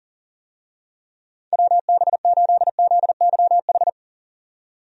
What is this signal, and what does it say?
Morse code, heard as a keyed tone, about 40 wpm, 700 Hz. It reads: WB8ZYH
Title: USA Call Signs - 6 Characters - 40wpm